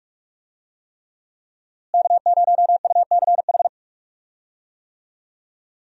Morse code, heard as a keyed tone, about 46 wpm, 700 Hz. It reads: K0UCH
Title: USA Call Signs - 5 Characters - Repeated 3x - 45wpm